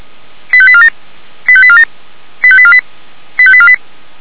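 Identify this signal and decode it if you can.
Five-tone selective call (ZVEI), heard as quick runs of five tones, tones of about 70 ms. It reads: 85847 85847 85848 85848
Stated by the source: ZVEI five-tone call 85752 received on an SDR, channel 464, from Feuerwehr Orlamünde.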